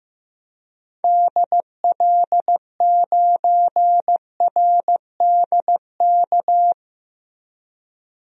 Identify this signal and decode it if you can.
Morse code, heard as a keyed tone, about 15 wpm, 700 Hz. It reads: DL9RDK
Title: DXCC-100 Call Signs - 6 Characters - Repeated 3x - 15wpm